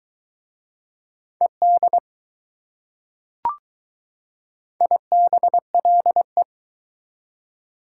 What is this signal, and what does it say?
Morse code, heard as a keyed tone, about 22 wpm, 700 Hz. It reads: ED IBLE